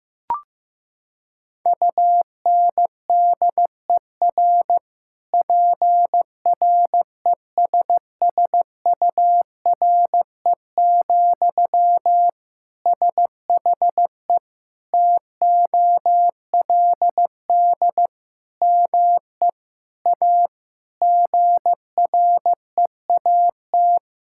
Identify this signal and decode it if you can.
Morse code, heard as a keyed tone, about 15 wpm, 700 Hz. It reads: UNDER PRESSURE, SHE TOLD ME A GREAT
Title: Sentences from Top 1000 Words - Review Part 6 - 15wpm